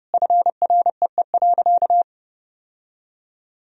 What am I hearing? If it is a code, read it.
Morse code, heard as a keyed tone, about 32 wpm, 700 Hz. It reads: FREE.
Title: Sentences from Top 1000 Words - Review Part 1 - 30wpm